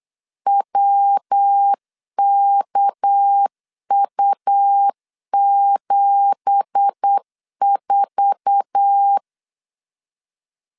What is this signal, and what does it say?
Morse code, heard as a keyed tone, about 8 wpm, 800 Hz. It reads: WKU74
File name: CW / DET / 1 9 1500C S B 8wpm 5624